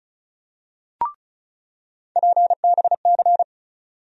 Morse code, heard as a keyed tone, about 36 wpm, 700 Hz. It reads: PBC
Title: USA Call Sign Postfixes - 3 Letters 35wpm